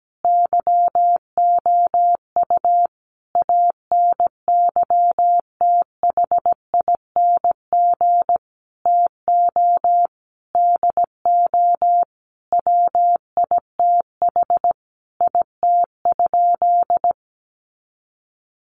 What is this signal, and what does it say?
Morse code, heard as a keyed tone, about 17 wpm, 700 Hz. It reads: YOU ANYTHING TO DO WITH IT?
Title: Sentences from Top 1000 Words 17wpm